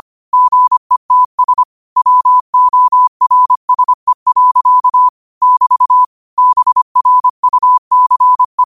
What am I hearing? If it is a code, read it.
Morse code, heard as a keyed tone, about 25 wpm, 1000 Hz. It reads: GETS WORSE. <BT> BRUCE